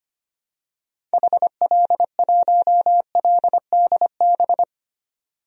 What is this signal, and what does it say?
Morse code, heard as a keyed tone, about 25 wpm, 700 Hz. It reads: HL1LDB
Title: DXCC-100 Call Signs - 6 Characters - 25wpm